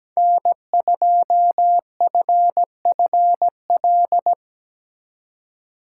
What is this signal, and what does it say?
Morse code, heard as a keyed tone, about 19 wpm, 700 Hz. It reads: N2FFL
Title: Speed Racing - USA 1x3 Call Signs 17wpm